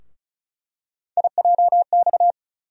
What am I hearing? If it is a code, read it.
Morse code, heard as a keyed tone, about 36 wpm, 700 Hz. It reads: IJX
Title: USA Call Sign Postfixes - 3 Letters 35wpm